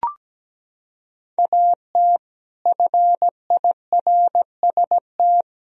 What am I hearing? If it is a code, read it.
Morse code, heard as a keyed tone, about 17 wpm, 700 Hz. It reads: AT FIRST
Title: Sentences from Top 1000 Words - Review Part 4 - 17wpm